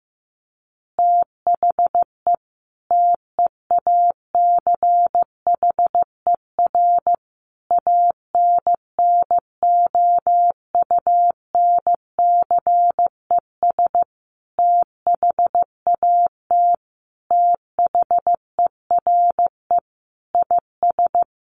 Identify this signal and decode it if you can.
Morse code, heard as a keyed tone, about 14 wpm, 700 Hz. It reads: THE TEACHER ANNOUNCES THAT THERE IS
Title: Additional Easy Phrases 15wpm